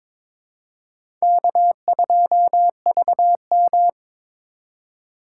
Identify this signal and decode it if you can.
Morse code, heard as a keyed tone, about 21 wpm, 700 Hz. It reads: K2VM
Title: USA Call Signs - 4 Characters - 4x Character Spacing - 22wpm